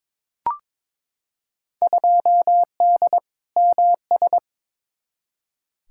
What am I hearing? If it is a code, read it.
Morse code, heard as a keyed tone, about 22 wpm, 700 Hz. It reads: 2D MS
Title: Field Day Exchanges Encore - 22wpm